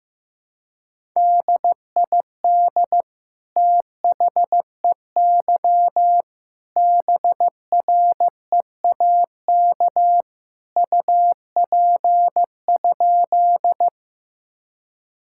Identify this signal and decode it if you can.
Morse code, heard as a keyed tone, about 15 wpm, 700 Hz. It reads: DID THEY BREAK UP?
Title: Sentences from Top 1000 Words - Review Part 6 - 15wpm